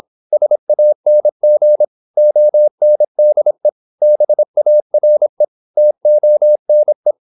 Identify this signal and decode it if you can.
Morse code, heard as a keyed tone, about 26 wpm, 600 Hz. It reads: SANG ONDE BARE TONE
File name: Korte ord DK 26wpm